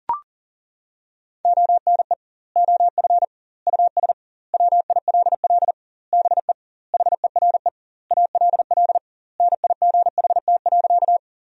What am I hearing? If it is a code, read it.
Morse code, heard as a keyed tone, about 40 wpm, 700 Hz. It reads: ONE OF US WILL BE HERE ALL NIGHT.
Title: 2x Word-Spacing - Sentences from Top 300 Words - 40wpm